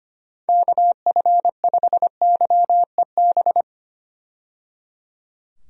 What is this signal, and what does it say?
Morse code, heard as a keyed tone, about 25 wpm, 700 Hz. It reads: KF5YEB
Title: USA Call Signs - 6 Characters - 25wpm